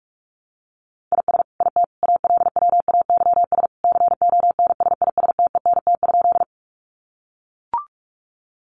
Morse code, heard as a keyed tone, about 44 wpm, 700 Hz. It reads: IS IT ALWAYS CONSISTENT?
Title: Sentences from Top 1500 Words 45wpm